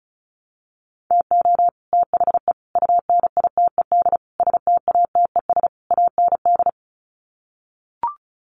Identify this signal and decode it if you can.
Morse code, heard as a keyed tone, about 34 wpm, 700 Hz. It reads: TO THE UNITED STATES AND
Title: Sets of 5 Words Encore 35wpm